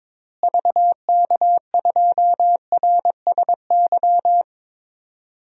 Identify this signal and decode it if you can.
Morse code, heard as a keyed tone, about 22 wpm, 700 Hz. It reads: VK2RSY
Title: DXCC-100 Call Signs - 6 Characters - Repeated 2x - 22wpm